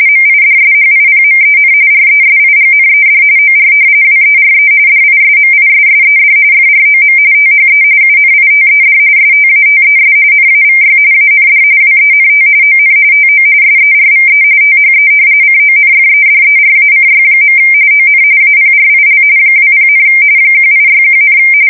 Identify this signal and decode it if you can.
RTTY, about 45.45 baud. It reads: MIDSHIPMEN RESIGNED OR WERE DISMISSED FROM THE UNITED STATES NAVY AND WENT ON TO SERVE THE CONFEDERACY. ON APRIL 20, 1861,
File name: fsk170 / usnavy3